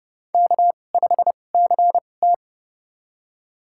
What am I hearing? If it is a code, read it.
Morse code, heard as a keyed tone, about 30 wpm, spaced as for 22 wpm, 700 Hz. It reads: K5CT
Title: USA Call Signs - 4 Characters - 2x Character Spacing - 30wpm